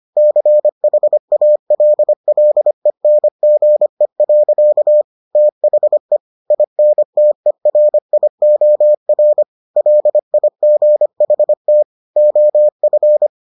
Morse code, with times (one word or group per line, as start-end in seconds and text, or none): 0.2-5.0 CHALLENGE.
5.4-6.2 THE
6.5-9.4 INTERIOR
9.8-11.8 LIGHT
12.2-13.3 OF